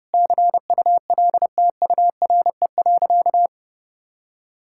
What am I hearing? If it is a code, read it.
Morse code, heard as a keyed tone, about 32 wpm, 700 Hz. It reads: CULTURE.
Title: Sentences from Top 1000 Words - Review Part 6 - 30wpm